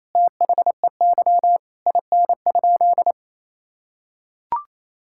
Morse code, heard as a keyed tone, about 28 wpm, 700 Hz. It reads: THEY IN?
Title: Sentences from Top 1000 Words - Review Part 5 - 28wpm